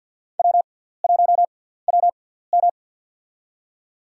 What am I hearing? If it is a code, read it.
Morse code, heard as a keyed tone, about 51 wpm, spaced as for 17 wpm, 690 Hz. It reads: W1WM
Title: USA Call Signs - 4 Characters - 6x Character Spacing - 50wpm